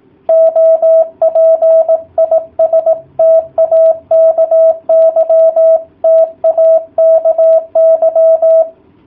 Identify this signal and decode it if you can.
Morse code, heard as a keyed tone, about 18 wpm, 650 Hz. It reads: OP IS TAKY TAKY